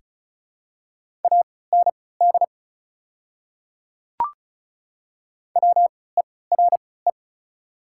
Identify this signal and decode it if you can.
Morse code, heard as a keyed tone, about 36 wpm, spaced as for 20 wpm, 690 Hz. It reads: AND WERE
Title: 3x Character Spacing - Top 100 Words - 35wpm